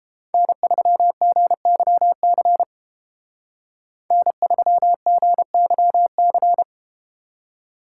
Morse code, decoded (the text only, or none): N3GYC N3GYC